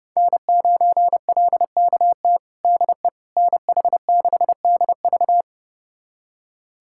Morse code, heard as a keyed tone, about 32 wpm, 700 Hz. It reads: N9LKT DE NH6DV
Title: USA Callsign DE Callsign - 5 Characters - 30wpm